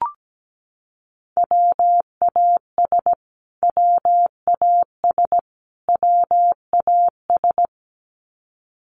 Morse code, heard as a keyed tone, about 17 wpm, 700 Hz. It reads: WAS WAS WAS
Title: Top 200 Words Repeated 17wpm